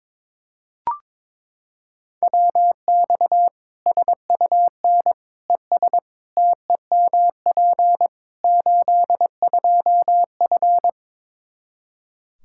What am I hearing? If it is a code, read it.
Morse code, heard as a keyed tone, about 22 wpm, 700 Hz. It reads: WX SUN ES TEMP 82F
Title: K1USN SST QSO Practice 22wpm